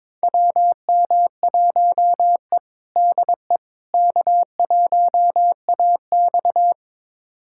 Morse code, heard as a keyed tone, about 21 wpm, 700 Hz. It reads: WM1E DE K1AX
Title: USA Callsign DE Callsign - 4 Characters - 22wpm